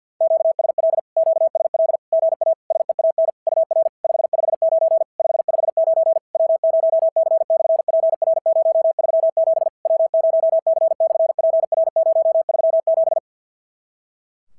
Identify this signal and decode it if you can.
Morse code, heard as a keyed tone, about 50 wpm, 650 Hz. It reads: QSL QSL GA SEAN UR 559 559 W0C/PR037 W0C/PR037
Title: SOTA QSO Practice 50wpm